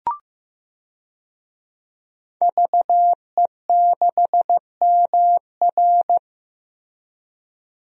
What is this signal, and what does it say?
Morse code, heard as a keyed tone, about 15 wpm, 700 Hz. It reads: VE6MR
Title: DXCC-100 Call Signs - 5 Characters - Repeated 3x - 15wpm